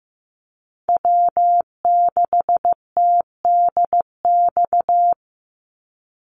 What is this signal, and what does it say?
Morse code, heard as a keyed tone, about 15 wpm, 700 Hz. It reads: W6TDX
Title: NA SOTA Chaser Callsigns 15wpm